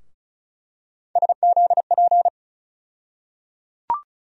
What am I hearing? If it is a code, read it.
Morse code, heard as a keyed tone, about 36 wpm, 700 Hz. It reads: SZP
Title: USA Call Sign Postfixes - 3 Letters 35wpm